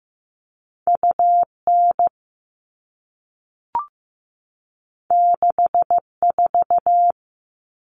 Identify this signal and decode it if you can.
Morse code, heard as a keyed tone, about 15 wpm, 700 Hz. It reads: UN 64
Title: All Cut Numbers - 2-Digit 15wpm